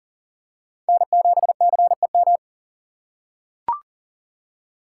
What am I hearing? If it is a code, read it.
Morse code, heard as a keyed tone, about 40 wpm, 700 Hz. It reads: N7CEM
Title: USA Call Signs - 5 Characters - 6x Character Spacing - 40wpm